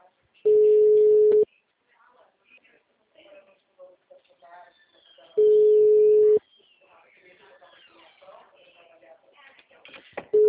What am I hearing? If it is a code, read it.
Ringback tone (Europe, 425 Hz), with speech.
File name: out- -2011- - - .27578